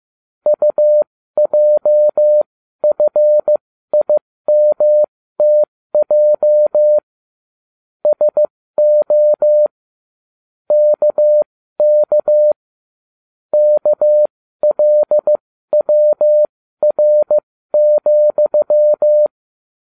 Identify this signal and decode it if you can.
Morse code, heard as a keyed tone, about 15 wpm, spaced as for 13 wpm, 600 Hz. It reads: UJFIMTJ SO KK KLWR,